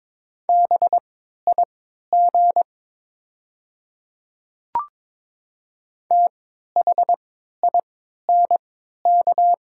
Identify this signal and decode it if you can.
Morse code, heard as a keyed tone, about 23 wpm, spaced as for 13 wpm, 700 Hz. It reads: BIG THINK
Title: 3x Character Spacing - Top 200 Words - 22wpm